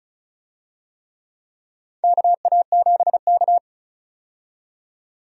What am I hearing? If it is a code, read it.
Morse code, heard as a keyed tone, about 34 wpm, 700 Hz. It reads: KA7K
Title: USA Call Signs - 4 Characters - 35wpm